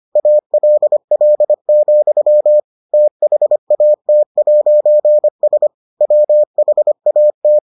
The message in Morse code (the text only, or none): ALL, THAT'S WHAT